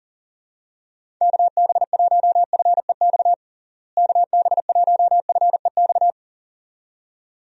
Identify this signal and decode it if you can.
Morse code, heard as a keyed tone, about 40 wpm, 700 Hz. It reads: KB1FEX KB1FEX
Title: USA Call Signs - 6 Characters - Repeated 2x - 40wpm